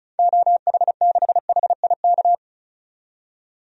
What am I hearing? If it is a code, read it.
Morse code, heard as a keyed tone, about 34 wpm, 700 Hz. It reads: OH6HIK
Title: DXCC-100 Call Signs - 6 Characters - 35wpm